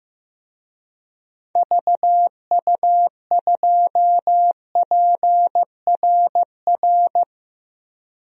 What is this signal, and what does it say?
Morse code, heard as a keyed tone, about 15 wpm, 700 Hz. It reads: VU2PRR
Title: DXCC-100 Call Signs - 6 Characters - 15wpm